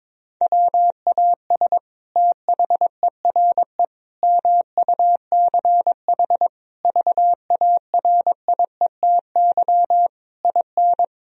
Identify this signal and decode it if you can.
Morse code, heard as a keyed tone, about 21 wpm, 700 Hz. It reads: WAS THERE MUCH VARIETY IN
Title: Sentences from Top 1000 Words - Review Part 4 - 22wpm